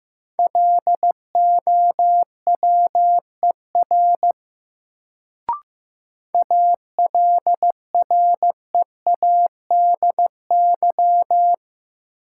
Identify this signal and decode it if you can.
Morse code, heard as a keyed tone, about 15 wpm, 700 Hz. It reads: LOWER ALREADY
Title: Sentences from Top 700 Words - New Word Drill - 15wpm